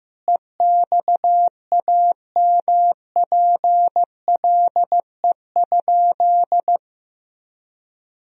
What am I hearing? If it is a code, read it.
Morse code, heard as a keyed tone, about 15 wpm, 700 Hz. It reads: EXAMPLE?
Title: Sentences from Top 1000 Words - Review Part 1 - 15wpm